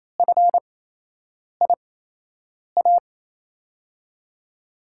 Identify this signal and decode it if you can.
Morse code, heard as a keyed tone, about 28 wpm, spaced as for 8 wpm, 700 Hz. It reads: FIA
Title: USA Call Signs - 6 Characters - 8x Character Spacing - 28wpm